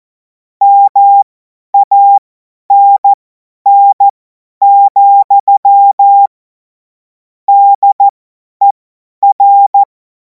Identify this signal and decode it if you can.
Morse code, heard as a keyed tone, about 14 wpm, spaced as for 10 wpm, 800 Hz. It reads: MANN, DER
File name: Morsekurs / Witze / Witz 10. Februar 2026 50 F70